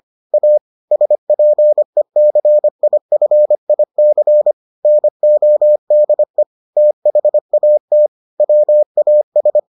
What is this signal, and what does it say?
Morse code, heard as a keyed tone, about 25 wpm, 600 Hz. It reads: A SPECIFIC NODE THAT WAS